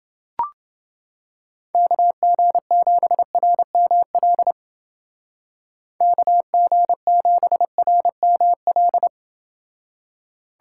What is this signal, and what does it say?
Morse code, heard as a keyed tone, about 29 wpm, 700 Hz. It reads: KG7RML KG7RML